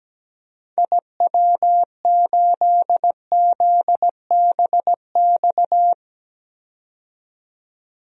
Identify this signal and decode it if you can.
Morse code, heard as a keyed tone, about 17 wpm, 700 Hz. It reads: IW8ZBX
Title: DXCC-100 Call Signs - 6 Characters - 17wpm